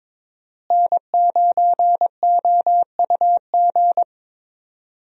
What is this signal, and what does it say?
Morse code, heard as a keyed tone, about 22 wpm, 700 Hz. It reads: N9OUG
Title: USA Call Signs - 5 Characters - Repeated 2x - 22wpm